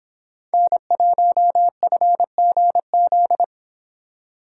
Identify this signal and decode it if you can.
Morse code, heard as a keyed tone, about 26 wpm, 700 Hz. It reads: N1FGZ